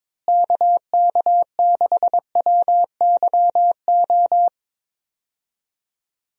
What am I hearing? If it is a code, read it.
Morse code, heard as a keyed tone, about 22 wpm, 700 Hz. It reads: KK6WYO
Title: USA Call Signs - 6 Characters - Repeated 2x - 22wpm